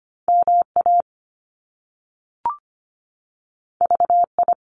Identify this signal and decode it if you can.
Morse code, heard as a keyed tone, about 25 wpm, 700 Hz. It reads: MA VI
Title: Top Two Letter Combinations in Words 25wpm